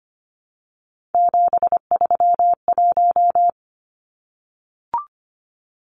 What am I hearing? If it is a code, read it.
Morse code, heard as a keyed tone, about 25 wpm, 700 Hz. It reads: 731
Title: Numbers - 3-Digit 25wpm